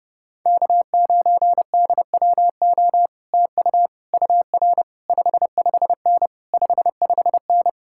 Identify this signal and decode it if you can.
Morse code, heard as a keyed tone, about 29 wpm, 700 Hz. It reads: K9DWO TU UR 55N 55N